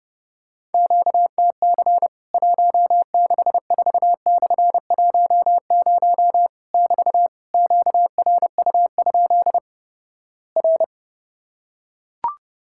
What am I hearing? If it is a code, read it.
Morse code, heard as a keyed tone, about 31 wpm, 700 Hz. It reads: QTC 164/10 <BT> QRU? R
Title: Worked All Europe DX Contest - QTC Exchange - 30wpm